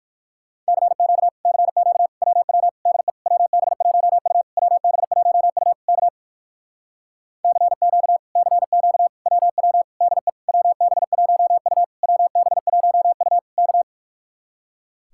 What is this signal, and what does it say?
Morse code, heard as a keyed tone, about 61 wpm, 700 Hz. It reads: CQ CQ WW DE WB1U WB1U K CQ CQ WW DE WB1U WB1U K